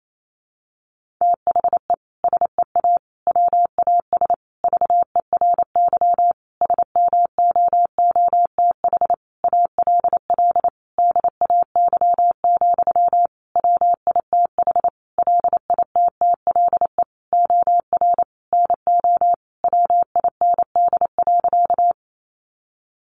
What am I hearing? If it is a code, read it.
Morse code, heard as a keyed tone, about 28 wpm, 700 Hz. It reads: THE SEA WAS VERY SMOOTH ALL DAY, WITH LITTLE OR NO WIND.
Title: Sentences from Top 1400 Words 28wpm